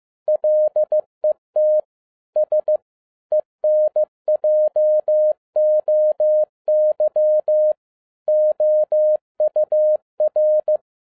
Morse code, heard as a keyed tone, about 15 wpm, 610 Hz. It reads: LET S ENJOY OUR
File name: 2011-07-mp3 / n02 JF2PDA 15wpm